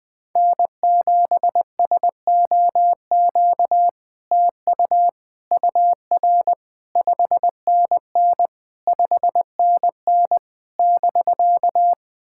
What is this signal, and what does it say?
Morse code, heard as a keyed tone, about 21 wpm, 700 Hz. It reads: N7SOQ TU UR 5NN 5NN <BK>
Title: SOTA QSO Practice 20wpm